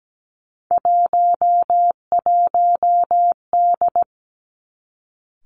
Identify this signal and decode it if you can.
Morse code, heard as a keyed tone, about 17 wpm, 700 Hz. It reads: 11D
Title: Three Challenging Letter-Number Combinations 17wpm